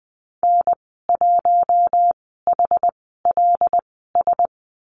Morse code, heard as a keyed tone, about 20 wpm, spaced as for 15 wpm, 700 Hz. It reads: N1HLS
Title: 076 Lesson 13g - Letter L - Call Signs 20wpm